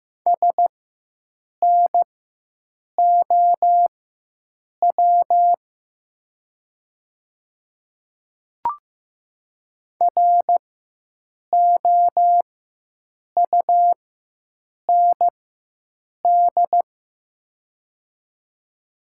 Morse code, heard as a keyed tone, about 15 wpm, spaced as for 7 wpm, 700 Hz. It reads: SNOW ROUND
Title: 4x Character Spacing - Top 500 Words - 15wpm